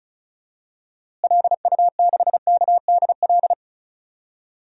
Morse code, heard as a keyed tone, about 36 wpm, 700 Hz. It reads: LU6KDL
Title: DXCC-100 Call Signs - 6 Characters - Repeated 3x - 35wpm